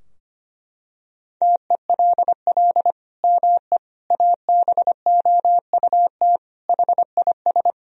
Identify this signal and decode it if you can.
Morse code, heard as a keyed tone, about 25 wpm, 700 Hz. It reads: TELL ME ABOUT HIS
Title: Sentences from Top 1000 Words - Review Part 3 - 25wpm